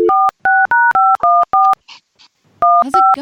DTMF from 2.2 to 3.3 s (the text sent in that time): 15